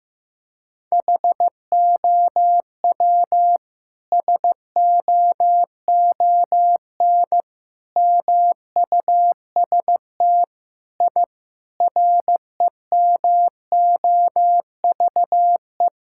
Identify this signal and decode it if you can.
Morse code, heard as a keyed tone, about 15 wpm, 700 Hz. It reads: HOW SOON MUST I REMOVE
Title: Sentences from Top 1000 Words - Review Part 5 - 15wpm